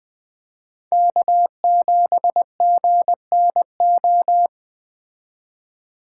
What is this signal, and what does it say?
Morse code, heard as a keyed tone, about 20 wpm, 700 Hz. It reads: K7GNO
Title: Speed Racing - USA 1x3 Call Signs 17wpm